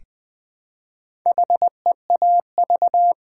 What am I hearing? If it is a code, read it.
Morse code, heard as a keyed tone, about 20 wpm, 700 Hz. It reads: HEAV
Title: Build A Word - 1 Letter At A Time - 5-Letter Words - 20wpm